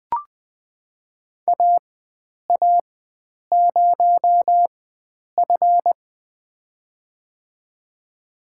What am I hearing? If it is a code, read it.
Morse code, heard as a keyed tone, about 21 wpm, spaced as for 9 wpm, 700 Hz. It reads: AA0F
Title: USA Call Signs - 4 Characters - 4x Character Spacing - 20wpm